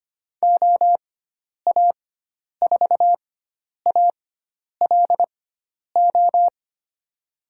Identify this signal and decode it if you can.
Morse code, heard as a keyed tone, about 25 wpm, spaced as for 10 wpm, 700 Hz. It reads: OA4ALO
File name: Lesson 16g - Letter c - Call Signs 25wpm